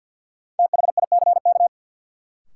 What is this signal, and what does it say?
Morse code, heard as a keyed tone, about 52 wpm, 700 Hz. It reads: THICK
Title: Sentences from Top 1600 Words - New Word Drill - 50wpm